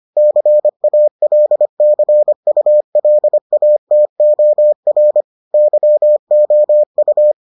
Morse code, heard as a keyed tone, about 25 wpm, 600 Hz. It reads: CALCULATOR YOU